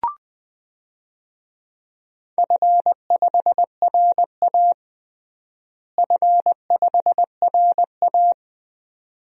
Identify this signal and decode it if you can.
Morse code, heard as a keyed tone, about 21 wpm, 700 Hz. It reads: F5RA F5RA
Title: DXCC-100 Call Signs - 4 Characters - Repeated 2x - 20wpm